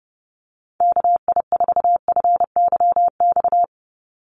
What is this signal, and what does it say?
Morse code, heard as a keyed tone, about 30 wpm, 700 Hz. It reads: KI4FYX
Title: USA Call Signs 30wpm